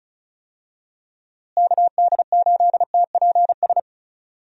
Morse code, heard as a keyed tone, about 34 wpm, 700 Hz. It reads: KD8TPS
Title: USA Call Signs - 6 Characters - Repeated 2x - 35wpm